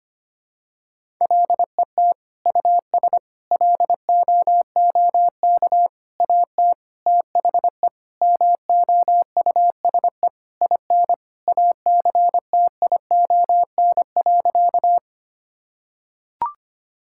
Morse code, heard as a keyed tone, about 25 wpm, 700 Hz. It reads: LET US LOOK AT THE MOUSE IN ACTION.
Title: Sentences from Top 1000 Words - Review Part 5 - 25wpm